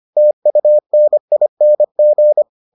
Morse code, heard as a keyed tone, about 25 wpm, 600 Hz. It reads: TUNING